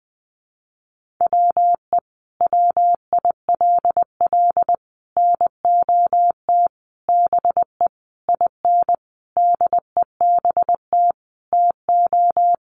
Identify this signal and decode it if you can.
Morse code, heard as a keyed tone, about 20 wpm, 700 Hz. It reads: WE WILL NOT BE IN DEBT TO
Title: Sentences from Top 1300 Words 20wpm